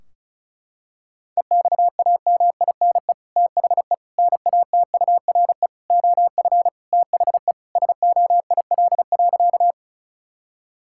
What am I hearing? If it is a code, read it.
Morse code, heard as a keyed tone, about 34 wpm, 700 Hz. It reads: EXAMINE THE NATURE OF THE SOIL.
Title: Sentences from Top 1600 Words 35wpm